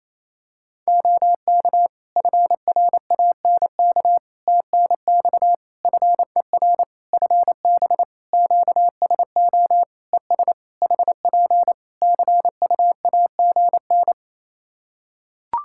Morse code, heard as a keyed tone, about 28 wpm, 700 Hz. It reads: OK FRANK TNX FER FB QSO ES HP CUAGN
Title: Practice QSO - Exchange 4 - Ending - 28wpm